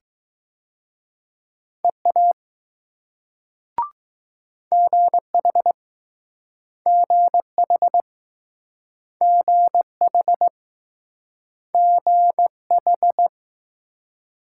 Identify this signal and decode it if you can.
Morse code, heard as a keyed tone, about 20 wpm, 700 Hz. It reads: EA GH GH GH GH